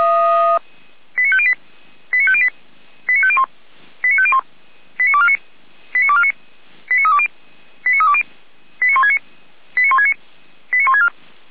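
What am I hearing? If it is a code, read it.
Five-tone selective call (ZVEI), heard as quick runs of five tones, tones of about 70 ms. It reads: 89508 89508 89501 89501 89249 89249 89320 89320 89169 89169 89164
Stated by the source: SDR-received ZVEI five-tone selective call, code 89560.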